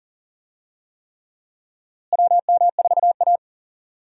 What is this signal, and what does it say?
Morse code, heard as a keyed tone, about 40 wpm, 700 Hz. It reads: WM4A
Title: USA Call Signs - 4 Characters - Repeated 3x - 40wpm